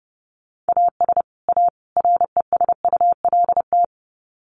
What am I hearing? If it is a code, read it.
Morse code, heard as a keyed tone, about 30 wpm, 700 Hz. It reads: AS A RESULT
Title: Sets of 3 Words 30wpm